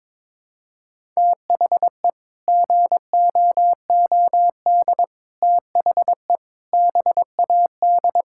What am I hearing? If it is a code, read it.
Morse code, heard as a keyed tone, about 22 wpm, 700 Hz. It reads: THE GOOD THE BAD
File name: practice / 22wpm / Sets of 6 Words - 22wpm